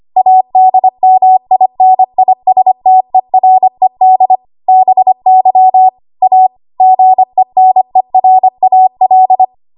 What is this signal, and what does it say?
Morse code, heard as a keyed tone, about 25 wpm, 750 Hz. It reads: ADMINISTERED BY A GENERAL
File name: Archive / 25 WPM / 25WPM